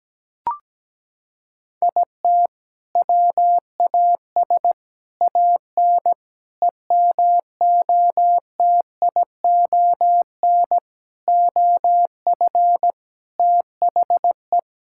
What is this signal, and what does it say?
Morse code, heard as a keyed tone, about 17 wpm, 700 Hz. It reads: IT WAS AN EMOTION OF THE
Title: Sentences from Top 1600 Words 17wpm